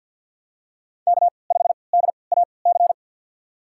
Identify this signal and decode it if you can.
Morse code, heard as a keyed tone, about 53 wpm, spaced as for 29 wpm, 700 Hz. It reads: K5DAC
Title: USA Call Signs - 5 Characters - 3x Character Spacing - 50wpm